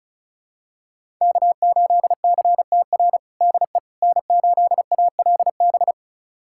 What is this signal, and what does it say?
Morse code, heard as a keyed tone, about 34 wpm, 700 Hz. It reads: K8CTR DE N8ALB
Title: USA Callsign DE Callsign - 5 Characters - 35wpm